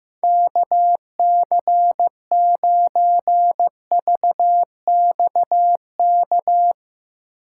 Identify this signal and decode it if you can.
Morse code, heard as a keyed tone, about 16 wpm, 700 Hz. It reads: KC9VXK
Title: Speed Racing - USA Call Signs 15wpm